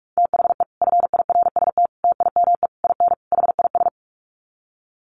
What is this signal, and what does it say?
Morse code, heard as a keyed tone, about 44 wpm, 700 Hz. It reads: THE FIRST TIME IN HIS
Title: Sets of 5 Words Encore2 45wpm